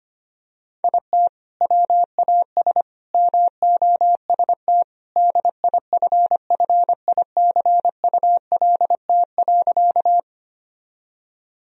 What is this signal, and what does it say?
Morse code, heard as a keyed tone, about 25 wpm, 700 Hz. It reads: IT WAS MOST DIFFICULT.
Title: Sentences from Top 1000 Words - Review Part 1 - 25wpm